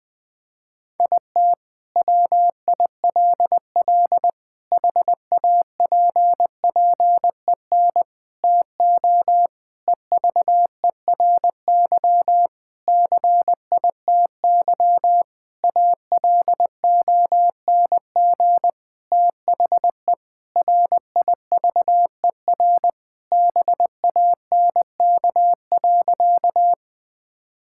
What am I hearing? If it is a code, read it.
Morse code, heard as a keyed tone, about 20 wpm, 700 Hz. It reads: IT WILL HAPPEN TO EVERY CITY ALONG THE RIVER BANK.
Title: Sentences from Top 1000 Words - Review Part 6 - 20wpm